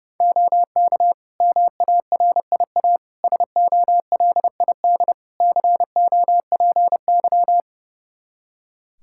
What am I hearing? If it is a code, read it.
Morse code, heard as a keyed tone, about 30 wpm, 700 Hz. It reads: OK MARIA SOLID COPY
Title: Practice QSO - Exchange 3 - More Info - 30wpm